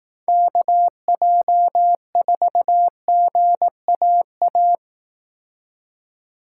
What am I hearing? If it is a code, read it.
Morse code, heard as a keyed tone, about 18 wpm, 700 Hz. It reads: KJ4GAA
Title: Speed Racing - USA 2x3 Call Signs 15wpm